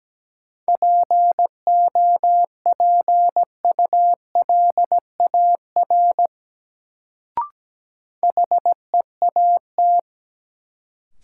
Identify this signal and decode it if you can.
Morse code, heard as a keyed tone, about 17 wpm, 700 Hz. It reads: POPULAR HEAT
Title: Sentences from Top 1000 Words - Review Part 3 - New Word Drill - 17wpm